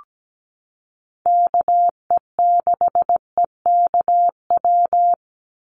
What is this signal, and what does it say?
Morse code, heard as a keyed tone, about 17 wpm, 700 Hz. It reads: KE6EKW
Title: USA Call Signs 17wpm